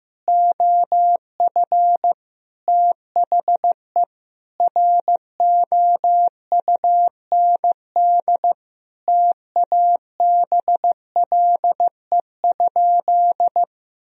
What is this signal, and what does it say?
Morse code, heard as a keyed tone, about 15 wpm, 700 Hz. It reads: OF THE ROUND TABLE?
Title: Sentences from Top 1000 Words - Review Part 2 - 15wpm